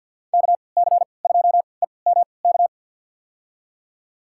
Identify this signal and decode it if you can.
Morse code, heard as a keyed tone, about 53 wpm, spaced as for 29 wpm, 700 Hz. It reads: KC2EMK
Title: USA Call Signs - 6 Characters - 3x Character Spacing - 50wpm